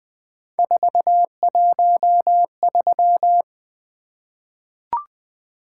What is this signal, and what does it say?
Morse code, heard as a keyed tone, about 20 wpm, 700 Hz. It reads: 413
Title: Three Letter-Number Combinations - 20wpm